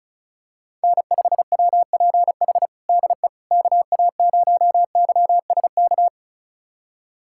Morse code, heard as a keyed tone, about 36 wpm, 700 Hz. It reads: N5WPH DE KA0YSK
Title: USA Callsign DE Callsign 35wpm